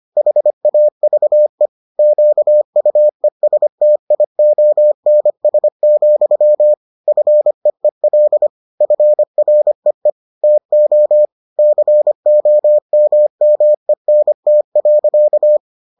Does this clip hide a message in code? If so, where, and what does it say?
Yes: Morse code from 0.2 to 15.6 s, heard as a keyed tone, about 25 wpm, 600 Hz. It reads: HAVE QUESTIONS, FEEL FREE TO COMMENT.